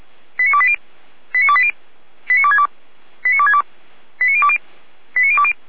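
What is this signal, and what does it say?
Five-tone selective call (ZVEI), heard as quick runs of five tones, tones of about 70 ms. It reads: 89280 89280 89262 89262 89020 89020
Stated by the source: ZVEI five-tone selective call 89333, received on SDR.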